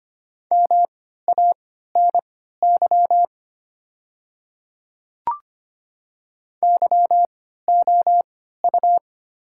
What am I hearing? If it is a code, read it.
Morse code, heard as a keyed tone, about 25 wpm, spaced as for 14 wpm, 700 Hz. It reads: MANY YOU
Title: 3x Character Spacing - Top 100 Words - 25wpm